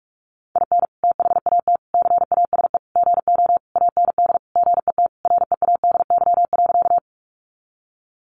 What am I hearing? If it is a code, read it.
Morse code, heard as a keyed tone, about 44 wpm, 700 Hz. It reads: IN THAT CASE GO AND GET READY.
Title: Sentences from Top 700 Words 45wpm